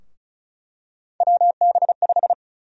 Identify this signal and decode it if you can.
Morse code, heard as a keyed tone, about 36 wpm, 700 Hz. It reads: WB5
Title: USA Call Sign Prefixes - Repeated 2x - 35wpm